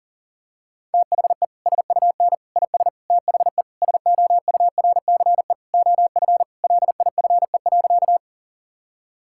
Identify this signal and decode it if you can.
Morse code, heard as a keyed tone, about 40 wpm, 690 Hz. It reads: THE SUN IS THE SOURCE OF LIFE.
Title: Sentences from Top 1000 Words - Review Part 5 - 40wpm